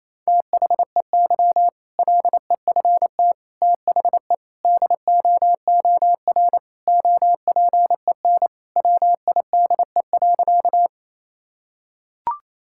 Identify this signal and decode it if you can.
Morse code, heard as a keyed tone, about 27 wpm, 700 Hz. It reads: THEY LEFT THE DOOR OPEN WIDE.
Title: Sentences from Top 1000 Words - Review Part 6 - 28wpm